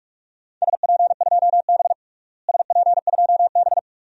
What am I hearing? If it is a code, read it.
Morse code, heard as a keyed tone, about 43 wpm, 700 Hz. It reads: SP2B SP2B
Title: DXCC-100 Call Signs - 4 Characters - Repeated 2x - 45wpm